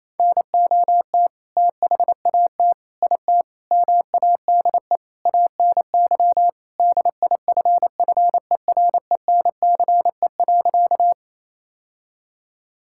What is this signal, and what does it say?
Morse code, heard as a keyed tone, about 27 wpm, 700 Hz. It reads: NOT THAT IT MADE ANY DIFFERENCE.
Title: Sentences from Top 1000 Words - Review Part 5 - 28wpm